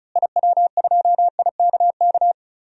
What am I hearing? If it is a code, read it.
Morse code, heard as a keyed tone, about 36 wpm, 700 Hz. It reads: IW2IKK
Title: DXCC-100 Call Signs - 6 Characters - Repeated 2x - 35wpm